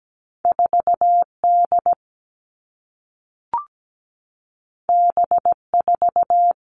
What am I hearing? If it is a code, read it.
Morse code, heard as a keyed tone, about 17 wpm, 700 Hz. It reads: 4D B4
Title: All Cut Numbers - 2-Digit 17wpm